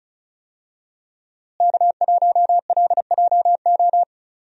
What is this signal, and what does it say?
Morse code, heard as a keyed tone, about 34 wpm, 700 Hz. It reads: K1LJO
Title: USA Call Signs - 5 Characters - Repeated 2x - 35wpm